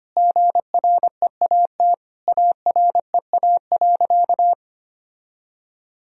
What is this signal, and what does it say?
Morse code, heard as a keyed tone, about 25 wpm, 700 Hz. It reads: GREAT AREA.
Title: Sentences from Top 1000 Words - Review Part 2 - 25wpm